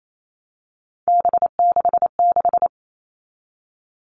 Morse code, heard as a keyed tone, about 28 wpm, 700 Hz. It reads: B66
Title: All Cut Numbers - 3-Digit 28wpm